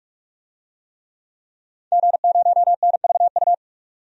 Morse code, heard as a keyed tone, about 46 wpm, 700 Hz. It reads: G0NVU
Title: DXCC-100 Call Signs - 5 Characters - 45wpm